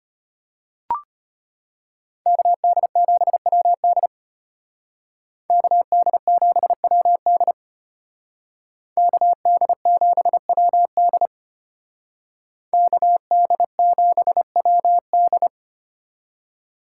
Morse code, heard as a keyed tone, about 32 wpm, 700 Hz. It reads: KD7WD KD7WD KD7WD KD7WD